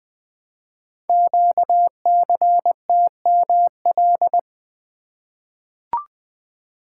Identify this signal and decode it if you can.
Morse code, heard as a keyed tone, about 20 wpm, 700 Hz. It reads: QCTML
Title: Code Groups - 5 Letters - 20wpm